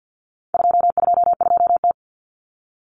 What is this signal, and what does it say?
Morse code, heard as a keyed tone, about 48 wpm, 700 Hz. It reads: 222T
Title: Common Cut Numbers - 4-Digit 50wpm